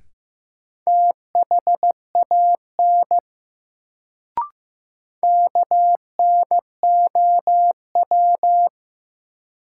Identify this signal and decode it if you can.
Morse code, heard as a keyed tone, about 15 wpm, 700 Hz. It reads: THAN KNOW
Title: Sentences from Top 100 Words - New Word Drill - 15wpm